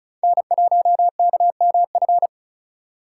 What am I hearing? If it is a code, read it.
Morse code, heard as a keyed tone, about 35 wpm, 700 Hz. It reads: N1KMF
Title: USA Call Signs - 5 Characters - 3x Character Spacing - 35wpm